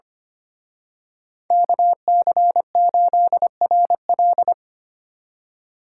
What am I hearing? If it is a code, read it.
Morse code, heard as a keyed tone, about 29 wpm, 700 Hz. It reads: KC8RL
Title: Speed Racing - USA 2x2 Call Signs 25wpm